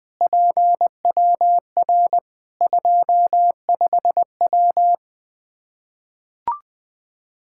Morse code, heard as a keyed tone, about 20 wpm, 700 Hz. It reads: PWR 25W
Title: Practice QSO - Exchange 2 - Info - 20wpm